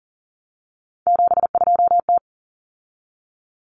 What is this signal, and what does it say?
Morse code, heard as a keyed tone, about 40 wpm, 700 Hz. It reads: 72T
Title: Common Cut Numbers - 3-Digit 40wpm